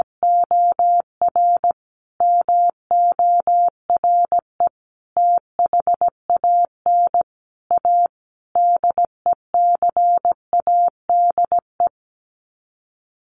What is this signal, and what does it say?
Morse code, heard as a keyed tone, about 17 wpm, 700 Hz. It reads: OR MORE THAN A DECADE
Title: Sets of 5 Words 17wpm